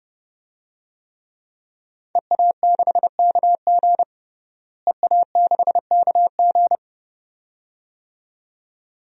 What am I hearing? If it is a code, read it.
Morse code, heard as a keyed tone, about 32 wpm, 700 Hz. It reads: EA6KG EA6KG
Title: DXCC-100 Call Signs - 5 Characters - Repeated 2x - 30wpm